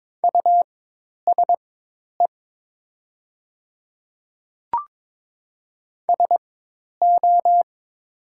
Morse code, heard as a keyed tone, about 22 wpm, spaced as for 10 wpm, 700 Hz. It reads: USE SO
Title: 4x Character Spacing - Top 100 Words - 22wpm